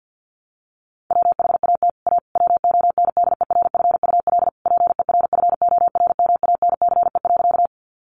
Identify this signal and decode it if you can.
Morse code, heard as a keyed tone, about 50 wpm, 700 Hz. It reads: WHAT A WONDERFUL PERFORMANCE.
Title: Sentences from Top 1100 Words 50wpm